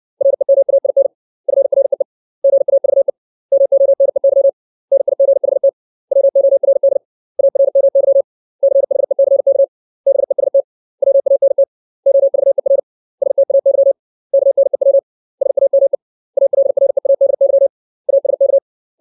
Morse code, heard as a keyed tone, about 60 wpm, 550 Hz. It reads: REMAIN FREE GAVE MONEY NIGHT WORD ARMY CHECK BEST WANT OVER STAY KNEW SAME ALREADY ASK
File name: 632 words / 632 Words 60WPM